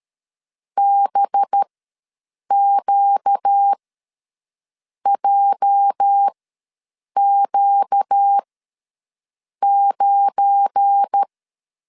Morse code, heard as a keyed tone, about 13 wpm, spaced as for 6 wpm, 800 Hz. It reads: BQJQ9
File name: LEZIONI / 10 SIGNS / 10B INT ws 12wpm 3527